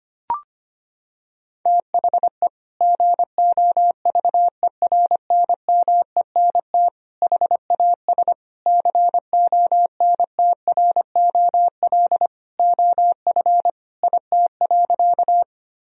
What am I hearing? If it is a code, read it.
Morse code, heard as a keyed tone, about 25 wpm, 700 Hz. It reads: THE GOVERNMENT HAS CONTROL OF IT.
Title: Sentences from Top 1000 Words - Review Part 3 - 25wpm